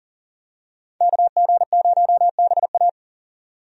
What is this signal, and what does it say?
Morse code, heard as a keyed tone, about 38 wpm, 700 Hz. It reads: KG0BA
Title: USA Call Signs - 5 Characters - 40wpm